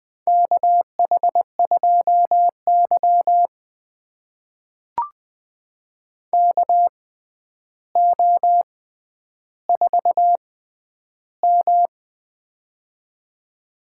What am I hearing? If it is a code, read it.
Morse code, heard as a keyed tone, about 20 wpm, spaced as for 10 wpm, 700 Hz. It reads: KH2Y KO4M
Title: USA Call Signs - 4 Characters - 6x Character Spacing - 20wpm